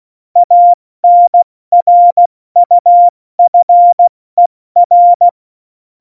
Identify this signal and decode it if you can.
Morse code, heard as a keyed tone, about 16 wpm, spaced as for 14 wpm, 700 Hz. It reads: ANRUFER
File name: Morsekurs / Witze / Witz 05. Juni 2018, 70 F80